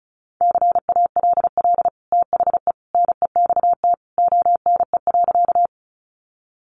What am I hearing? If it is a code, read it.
Morse code, heard as a keyed tone, about 34 wpm, 700 Hz. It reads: CALL THE NEXT ONE.
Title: Sentences from Top 300 Words 35wpm